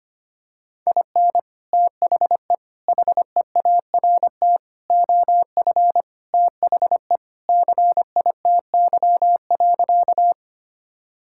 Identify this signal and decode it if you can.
Morse code, heard as a keyed tone, about 25 wpm, 700 Hz. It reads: IN THE HEART OF THE CITY.
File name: Sentences with 3-Word Phrases - 25wpm